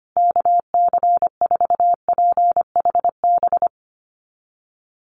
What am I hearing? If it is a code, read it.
Morse code, heard as a keyed tone, about 25 wpm, 700 Hz. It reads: KC4PHB
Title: USA Call Signs Repeated 2x 25wpm